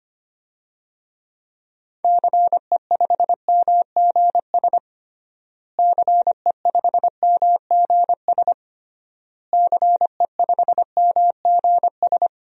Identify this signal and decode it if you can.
Morse code, heard as a keyed tone, about 25 wpm, 700 Hz. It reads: CE5MGS CE5MGS CE5MGS
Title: DXCC-100 Call Signs - 6 Characters - Repeated 3x - 25wpm